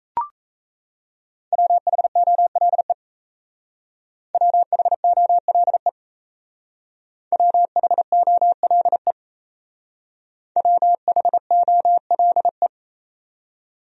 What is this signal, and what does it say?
Morse code, heard as a keyed tone, about 37 wpm, 700 Hz. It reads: WHOLE WHOLE WHOLE WHOLE